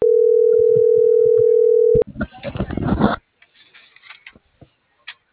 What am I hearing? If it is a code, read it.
Ringback tone (North America), 440 + 480 Hz.